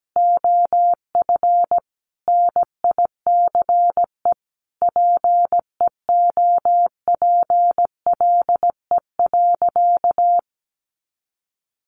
Morse code, heard as a keyed tone, about 17 wpm, 700 Hz. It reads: OF NICE PEOPLE.
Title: Sentences from Top 1100 Words 17wpm